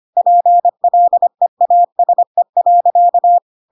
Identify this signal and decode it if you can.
Morse code, heard as a keyed tone, about 25 wpm, 700 Hz. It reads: PLEASE.
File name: Pinocchio / Pinocchio ch4 25 25 ik1hge